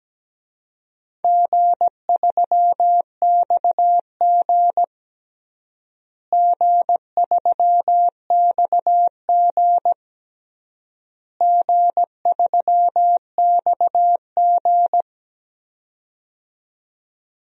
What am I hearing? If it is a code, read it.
Morse code, heard as a keyed tone, about 17 wpm, 700 Hz. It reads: G3XG G3XG G3XG
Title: DXCC-100 Call Signs - 4 Characters - Repeated 3x - 17wpm